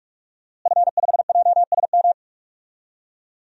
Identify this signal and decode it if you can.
Morse code, heard as a keyed tone, about 46 wpm, 700 Hz. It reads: F5JSM
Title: DXCC-100 Call Signs - 5 Characters - Repeated 3x - 45wpm